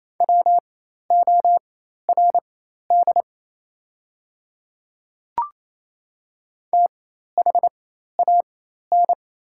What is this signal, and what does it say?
Morse code, heard as a keyed tone, about 27 wpm, spaced as for 13 wpm, 690 Hz. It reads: WORD THAN
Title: 4x Character Spacing - Top 100 Words - 28wpm